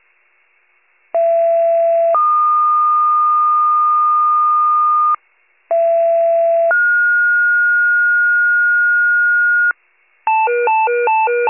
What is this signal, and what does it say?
Two-tone page, heard as a one-second tone, then a three-second tone, about 670 then 1220 Hz, about 670 then 1430 Hz, with noise. It is heard from 1.1 to 9.7 s.
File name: wcalert